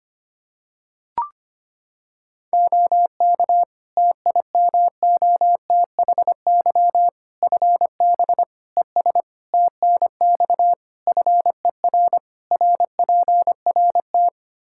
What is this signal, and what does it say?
Morse code, heard as a keyed tone, about 25 wpm, 700 Hz. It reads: OK TIMOTHY FB ES TNX FER RPRT
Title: K1USN SST QSO Practice 25wpm